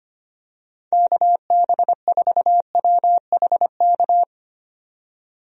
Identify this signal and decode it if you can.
Morse code, heard as a keyed tone, about 25 wpm, 700 Hz. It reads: KB4WHK
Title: USA Call Signs - 6 Characters - 25wpm